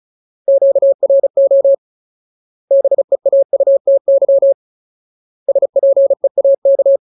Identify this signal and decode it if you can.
Morse code, heard as a keyed tone, about 34 wpm, 550 Hz. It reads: QRO BEAUTY SPEAK
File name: Ultimate Word List / 35WPM Ultimate